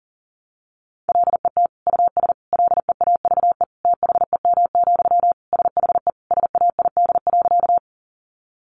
Morse code, heard as a keyed tone, about 40 wpm, 700 Hz. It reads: LET US LEAVE THEM, SHE SAID.
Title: Sentences from Top 400 Words 40wpm